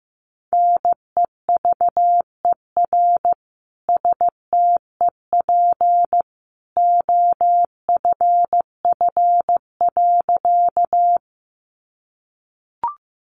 Morse code, heard as a keyed tone, about 15 wpm, 700 Hz. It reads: NEVER STEP OFF.
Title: A Sound Of Thunder 15wpm